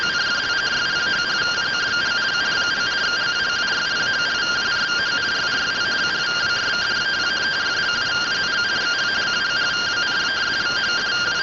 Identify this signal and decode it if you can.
Two-tone teleprinter signal, about 170 Hz shift.